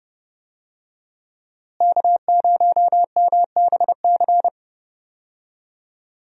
Morse code, heard as a keyed tone, about 30 wpm, 700 Hz. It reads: K0MBC
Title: USA Call Signs - 5 Characters - Repeated 3x - 30wpm